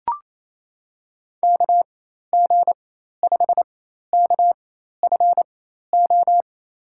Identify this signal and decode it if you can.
Morse code, heard as a keyed tone, about 28 wpm, spaced as for 13 wpm, 700 Hz. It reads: KG5KFO
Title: USA Call Signs - 6 Characters - 4x Character Spacing - 28wpm